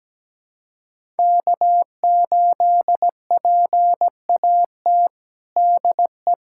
Morse code, heard as a keyed tone, about 17 wpm, 700 Hz. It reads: K8PAT DE
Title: USA Callsign DE Callsign - 5 Characters - 17wpm